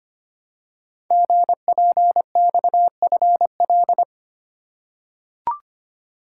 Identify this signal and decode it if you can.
Morse code, heard as a keyed tone, about 25 wpm, 700 Hz. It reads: GPXFL
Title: Code Groups - 5 Letters - 25wpm